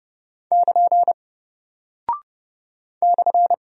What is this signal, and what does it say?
Morse code, heard as a keyed tone, about 30 wpm, 700 Hz. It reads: <KN> /
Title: Single Character (Letter:Number:Punctuation:Prosign) Alt 30wpm